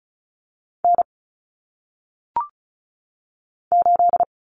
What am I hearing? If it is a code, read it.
Morse code, heard as a keyed tone, about 36 wpm, 700 Hz. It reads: N 8
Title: Common Cut Numbers - 1-Digit 35wpm